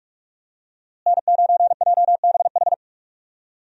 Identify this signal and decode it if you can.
Morse code, heard as a keyed tone, about 45 wpm, 700 Hz. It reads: N9JBH
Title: USA Call Signs - 5 Characters - Repeated 2x - 45wpm